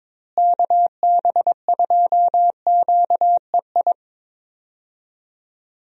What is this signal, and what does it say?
Morse code, heard as a keyed tone, about 22 wpm, 700 Hz. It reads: KB2QEI
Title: USA Call Signs - 6 Characters - Repeated 2x - 22wpm